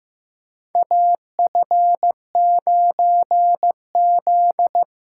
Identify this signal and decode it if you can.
Morse code, heard as a keyed tone, about 16 wpm, 700 Hz. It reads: AF9Z
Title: USA Call Signs - 4 Characters - 3x Character Spacing - 15wpm